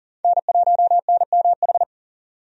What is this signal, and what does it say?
Morse code, heard as a keyed tone, about 40 wpm, 700 Hz. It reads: N1NMH
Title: USA Call Signs - 5 Characters - 40wpm